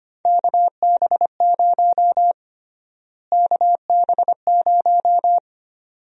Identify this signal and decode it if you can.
Morse code, heard as a keyed tone, about 25 wpm, 700 Hz. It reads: KB0 KB0
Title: USA Call Sign Prefixes - Repeated 2x - 25wpm